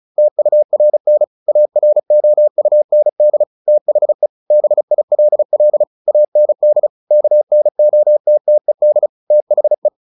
Morse code, heard as a keyed tone, about 34 wpm, 600 Hz. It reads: TURN AROUND THE BILL AND KNOTTED THE
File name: the old man and the sea 2 35wpm